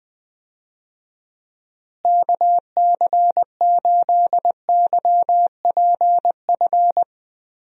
Morse code, heard as a keyed tone, about 20 wpm, 700 Hz. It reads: KC8YPF
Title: USA Call Signs - 6 Characters - Repeated 3x - 20wpm